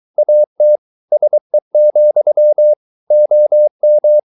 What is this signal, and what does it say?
Morse code, heard as a keyed tone, about 23 wpm, 600 Hz. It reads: AT SE, OM
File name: Ferie QRS 23wpm